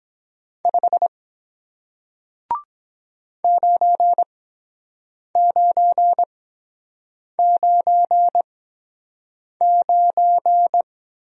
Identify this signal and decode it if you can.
Morse code, heard as a keyed tone, about 23 wpm, spaced as for 7 wpm, 700 Hz. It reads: 5 9999